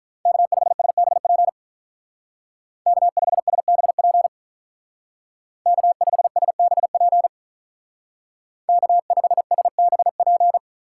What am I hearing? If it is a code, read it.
Morse code, heard as a keyed tone, about 45 wpm, 700 Hz. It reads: K5SBP K5SBP K5SBP K5SBP